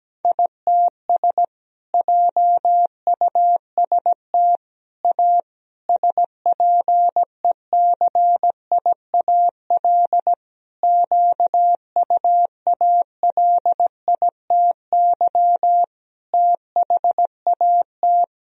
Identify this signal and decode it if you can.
Morse code, heard as a keyed tone, about 17 wpm, 700 Hz. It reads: ITS JUST A SPECIAL QUALITY THAT
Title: Sentences from Top 1000 Words - Review Part 4 - 17wpm